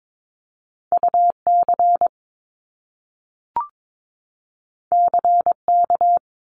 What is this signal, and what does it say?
Morse code, heard as a keyed tone, about 22 wpm, 700 Hz. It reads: UC CK
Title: Top Two Letter Combinations in Words 22wpm